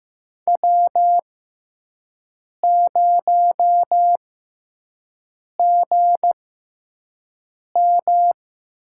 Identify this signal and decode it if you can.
Morse code, heard as a keyed tone, about 15 wpm, spaced as for 5 wpm, 700 Hz. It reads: W0GM
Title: USA Call Signs - 4 Characters - 6x Character Spacing - 15wpm